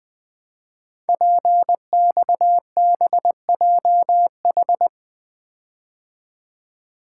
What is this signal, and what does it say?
Morse code, heard as a keyed tone, about 20 wpm, 700 Hz. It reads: PXBJH
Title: Code Groups - 5 Letters - 20wpm